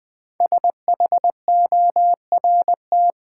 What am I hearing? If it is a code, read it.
Morse code, heard as a keyed tone, about 20 wpm, 700 Hz. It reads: SHORT
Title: Sets of 5 Words Encore - New Word Drill - 20wpm